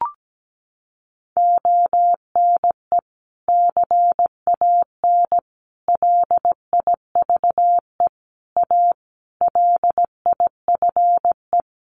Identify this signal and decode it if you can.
Morse code, heard as a keyed tone, about 17 wpm, 700 Hz. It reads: ONE CAN LIVE A LIFE
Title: Sentences from Top 1400 Words 17wpm